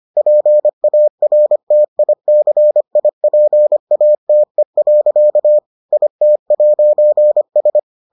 Morse code, heard as a keyed tone, about 25 wpm, 600 Hz. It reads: PARTICIPATE. IT'S